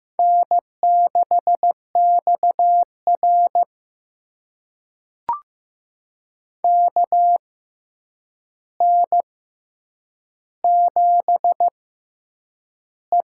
Morse code, heard as a keyed tone, about 15 wpm, spaced as for 8 wpm, 700 Hz. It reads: N6XR KN7E
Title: USA Call Signs - 4 Characters - 6x Character Spacing - 15wpm